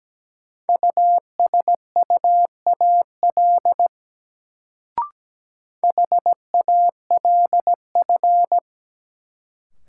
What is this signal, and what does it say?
Morse code, heard as a keyed tone, about 17 wpm, 700 Hz. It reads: USUAL HALF
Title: Sentences from Top 400 Words - New Word Drill - 17wpm